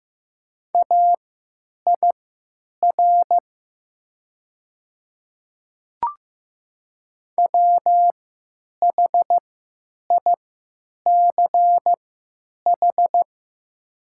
Morse code, heard as a keyed tone, about 15 wpm, spaced as for 9 wpm, 700 Hz. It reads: AIR WHICH
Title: 3x Character Spacing - Top 200 Words - 15wpm